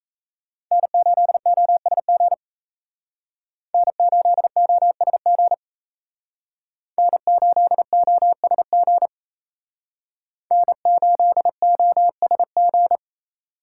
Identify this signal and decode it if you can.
Morse code, heard as a keyed tone, about 35 wpm, 700 Hz. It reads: N8OSG N8OSG N8OSG N8OSG